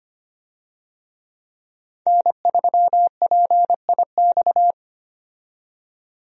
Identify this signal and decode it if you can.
Morse code, heard as a keyed tone, about 25 wpm, 690 Hz. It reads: N3PIX
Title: USA Call Signs - 5 Characters - 25wpm